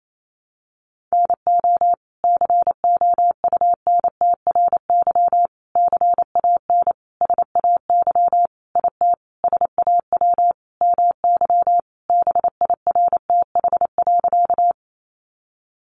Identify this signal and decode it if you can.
Morse code, heard as a keyed tone, about 28 wpm, 700 Hz. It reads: NO COUNTRY CAN SAY IT SAW MY BIRTH.
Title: Sentences from Top 800 Words 28wpm